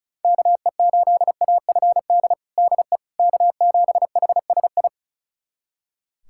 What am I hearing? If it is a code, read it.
Morse code, heard as a keyed tone, about 34 wpm, 700 Hz. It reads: KE8AFD DE K7HSI
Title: USA Callsign DE Callsign 35wpm